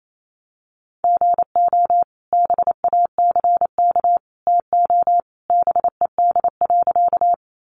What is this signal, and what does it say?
Morse code, heard as a keyed tone, about 28 wpm, 700 Hz. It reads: GO BACK TO BED.
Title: Sentences from Top 500 Words 28wpm